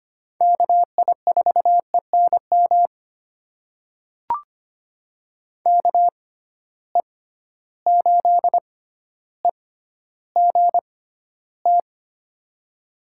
Morse code, heard as a keyed tone, about 25 wpm, spaced as for 13 wpm, 700 Hz. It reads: KI4ENM KE8EGT
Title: USA Call Signs - 6 Characters - 6x Character Spacing - 25wpm